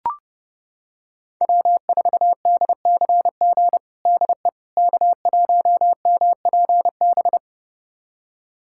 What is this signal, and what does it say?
Morse code, heard as a keyed tone, about 30 wpm, 700 Hz. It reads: W4DCG DE K1MPB
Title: USA Callsign DE Callsign - 5 Characters - 30wpm